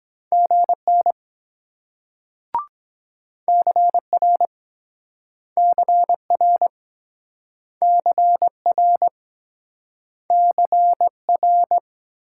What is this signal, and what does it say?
Morse code, heard as a keyed tone, about 23 wpm, 700 Hz. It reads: GN CR CR CR CR